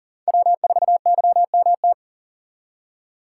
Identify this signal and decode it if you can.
Morse code, heard as a keyed tone, about 40 wpm, 700 Hz. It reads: W4YMT
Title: USA Call Signs - 5 Characters - 2x Character Spacing - 40wpm